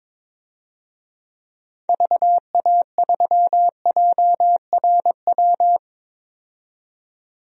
Morse code, heard as a keyed tone, about 21 wpm, 700 Hz. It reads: VA3JRW
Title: DXCC-100 Call Signs - 6 Characters - 22wpm